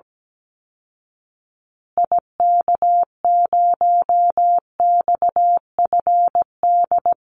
Morse code, heard as a keyed tone, about 17 wpm, 700 Hz. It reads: IK0XFD
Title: Call Signs Repeated 3x - 17wpm